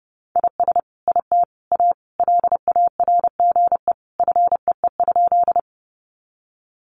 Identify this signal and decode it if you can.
Morse code, heard as a keyed tone, about 30 wpm, 700 Hz. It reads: IS IT A LARGE FEE?
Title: Sentences from Top 1500 Words 30wpm